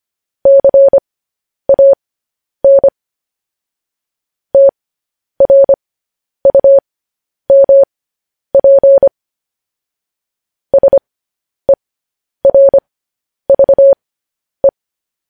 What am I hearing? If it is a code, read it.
Morse code, heard as a keyed tone, about 25 wpm, spaced as for 10 wpm, 550 Hz. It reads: CAN TRUMP SERVE
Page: news-headlines-in-morse-code-at-10-wpm